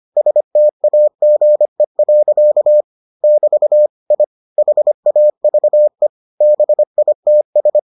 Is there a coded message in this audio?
Yes: Morse code from 0.2 to 7.8 s, heard as a keyed tone, about 25 wpm, 600 Hz. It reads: STAGE. <BT> I HAVE BITS